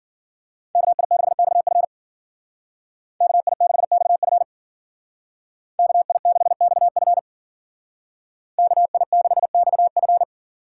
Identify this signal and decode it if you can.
Morse code, heard as a keyed tone, about 52 wpm, 700 Hz. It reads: KI6XF KI6XF KI6XF KI6XF